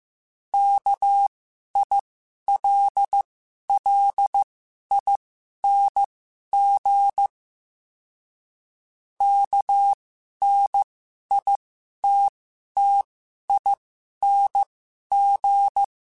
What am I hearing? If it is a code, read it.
Morse code, heard as a keyed tone, about 15 wpm, spaced as for 11 wpm, 800 Hz. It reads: KILLING KNITTING
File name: Special INGs-mp3s / Special INGs-mp3s / Sngl Ing @10